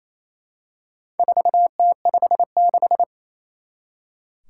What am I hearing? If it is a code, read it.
Morse code, heard as a keyed tone, about 28 wpm, 700 Hz. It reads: 4T56
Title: Four Challenging Letter-Number Combinations 28wpm